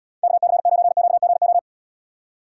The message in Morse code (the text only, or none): KC9QGO